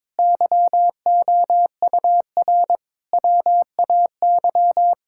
Morse code, heard as a keyed tone, about 22 wpm, 700 Hz. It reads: YOUR WAY
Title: Sentences from Top 1600 Words 22wpm